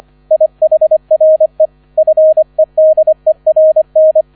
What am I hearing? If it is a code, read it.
Morse code, heard as a keyed tone, about 25 wpm, 630 Hz. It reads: IHRE FEDERN